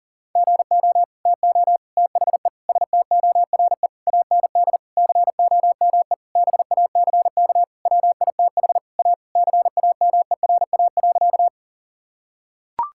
Morse code, heard as a keyed tone, about 40 wpm, 700 Hz. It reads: GO TO THE STORE AND COME BACK WITH A CAMERA.
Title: Sentences from Top 1000 Words - Review Part 5 - 40wpm